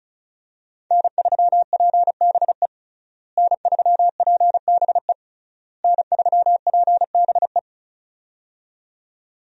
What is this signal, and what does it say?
Morse code, heard as a keyed tone, about 36 wpm, 700 Hz. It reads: N3PBE N3PBE N3PBE
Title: USA Call Signs - 5 Characters - Repeated 3x - 35wpm